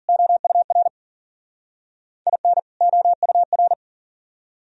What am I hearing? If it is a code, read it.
Morse code, heard as a keyed tone, about 49 wpm, 700 Hz. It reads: OUR IN OUR
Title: Speed-Racing - Sets of 2 Words 40wpm